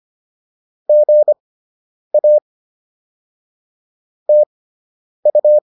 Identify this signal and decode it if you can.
Morse code, heard as a keyed tone, about 25 wpm, spaced as for 9 wpm, 600 Hz. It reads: GA TU